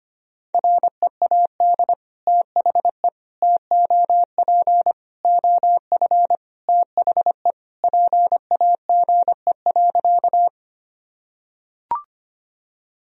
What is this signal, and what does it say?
Morse code, heard as a keyed tone, about 25 wpm, 700 Hz. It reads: READ THE TOP OF THE PAGE.
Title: Sentences from Top 1000 Words - Review Part 2 - 25wpm